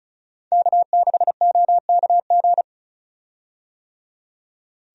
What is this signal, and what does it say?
Morse code, heard as a keyed tone, about 36 wpm, 700 Hz. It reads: K6OKG
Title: USA Call Signs - 5 Characters - 35wpm